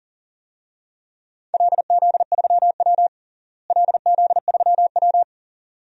Morse code, heard as a keyed tone, about 40 wpm, 700 Hz. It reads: LZ3W LZ3W
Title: DXCC-100 Call Signs - 4 Characters - Repeated 2x - 40wpm